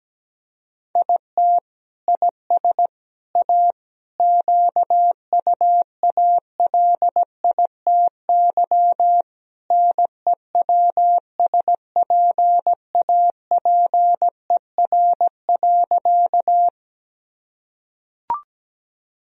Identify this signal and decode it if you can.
Morse code, heard as a keyed tone, about 17 wpm, 700 Hz. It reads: IT IS A QUALITY NEWSPAPER.
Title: Sentences from Top 1000 Words - Review Part 4 - 17wpm